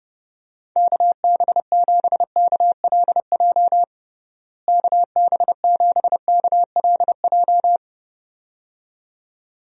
Morse code, heard as a keyed tone, about 30 wpm, 700 Hz. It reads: KB7KLJ KB7KLJ
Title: USA Call Signs - 6 Characters - Repeated 2x - 30wpm